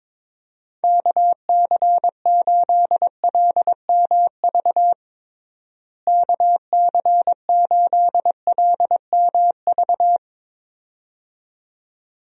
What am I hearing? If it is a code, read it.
Morse code, heard as a keyed tone, about 22 wpm, 700 Hz. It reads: KC8LMV KC8LMV
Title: USA Call Signs - 6 Characters - Repeated 2x - 22wpm